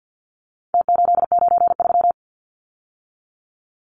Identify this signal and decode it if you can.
Morse code, heard as a keyed tone, about 49 wpm, 700 Hz. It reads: T893
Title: Numbers - Contest Style - 4-Digit 50wpm